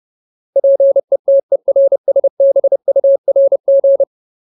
Morse code, heard as a keyed tone, about 30 wpm, 550 Hz. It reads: PETERSBURG
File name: Ultimate Word List / 30WPM Ultimate